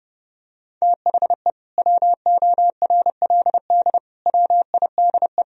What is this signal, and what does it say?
Morse code, heard as a keyed tone, about 30 wpm, 700 Hz. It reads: THE WORLD WIDE
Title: Sentences from Top 1000 Words - Review Part 4 - 30wpm